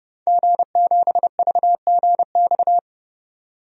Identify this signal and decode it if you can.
Morse code, heard as a keyed tone, about 30 wpm, 700 Hz. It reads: G7VGX
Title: DXCC-100 Call Signs - 5 Characters - 30wpm